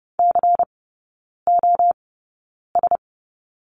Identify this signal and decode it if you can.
Morse code, heard as a keyed tone, about 30 wpm, spaced as for 9 wpm, 690 Hz. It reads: COS
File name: Lesson 16e - Letter c - Three Characters 30wpm